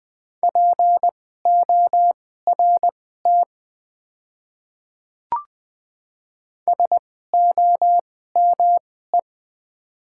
Morse code, heard as a keyed tone, about 20 wpm, spaced as for 14 wpm, 700 Hz. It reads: PORT SOME
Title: 2x Character Spacing - Top 200 Words - 20wpm